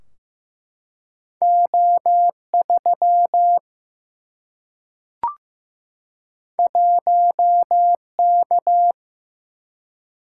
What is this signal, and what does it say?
Morse code, heard as a keyed tone, about 15 wpm, 700 Hz. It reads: O3 1K
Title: Single-Letter Single-Number 15wpm